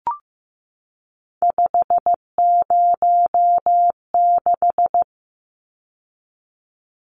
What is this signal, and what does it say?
Morse code, heard as a keyed tone, about 15 wpm, 700 Hz. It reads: 506
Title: Numbers - 3-Digit 15wpm